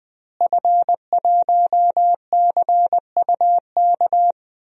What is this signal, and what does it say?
Morse code, heard as a keyed tone, about 21 wpm, 700 Hz. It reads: F1CUK
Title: DXCC-100 Call Signs - 5 Characters - Repeated 3x - 20wpm